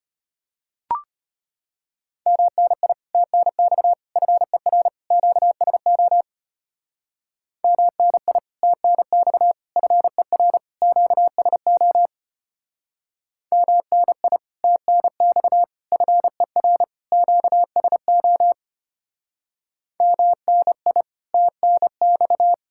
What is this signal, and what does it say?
Morse code, heard as a keyed tone, about 31 wpm, 700 Hz. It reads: MNI TNX FER QSO MNI TNX FER QSO MNI TNX FER QSO MNI TNX
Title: Speed-Racing Common QSO Phrases Encore - 25wpm